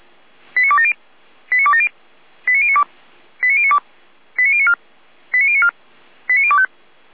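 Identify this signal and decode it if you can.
Five-tone selective call (ZVEI), heard as quick runs of five tones, tones of about 70 ms. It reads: 89270 89270 89092 89092 89094 89094 89025
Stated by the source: ZVEI five-tone call received on an SDR.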